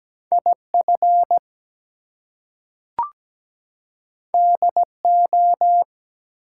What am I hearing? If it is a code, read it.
Morse code, heard as a keyed tone, about 17 wpm, 700 Hz. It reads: IF DO
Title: Two Letter Words - 17wpm